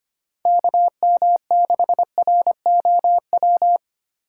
Morse code, heard as a keyed tone, about 25 wpm, 700 Hz. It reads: KM6ROW
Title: USA Call Signs - 6 Characters - 8x Character Spacing - 25wpm